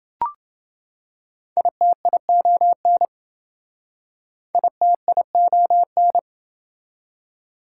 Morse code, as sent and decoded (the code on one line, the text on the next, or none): .. - .. --- -. / .. - .. --- -.
ITION ITION